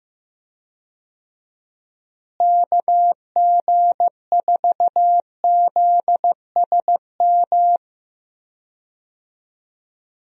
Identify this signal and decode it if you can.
Morse code, heard as a keyed tone, about 16 wpm, 700 Hz. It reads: KG4ZSM
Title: USA Call Signs - 6 Characters - Repeated 2x - 15wpm